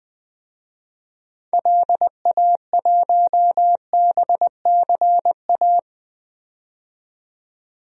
Morse code, heard as a keyed tone, about 21 wpm, 700 Hz. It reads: LA1BCA
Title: DXCC-100 Call Signs - 6 Characters - 20wpm